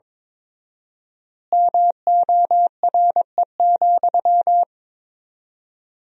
Morse code, heard as a keyed tone, about 21 wpm, 700 Hz. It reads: MORE,
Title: 4x Word-Spacing - Sentences from Top 300 Words - 22wpm